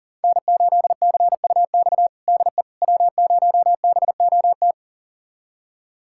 Morse code, heard as a keyed tone, about 40 wpm, 700 Hz. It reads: N8CUX DE W0BOT
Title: USA Callsign DE Callsign - 5 Characters - 40wpm